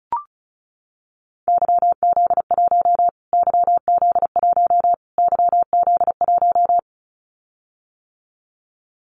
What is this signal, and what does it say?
Morse code, heard as a keyed tone, about 34 wpm, 700 Hz. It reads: YZ1 YZ1 YZ1
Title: Callsign Prefixes for Top 25 DXCC - Repeated 3x - 35wpm